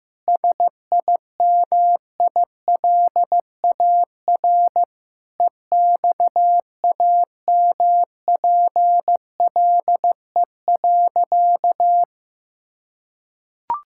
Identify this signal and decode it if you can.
Morse code, heard as a keyed tone, about 15 wpm, 700 Hz. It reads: SIMILAR EXAMPLE.
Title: Sentences from Top 1000 Words - Review Part 5 - 15wpm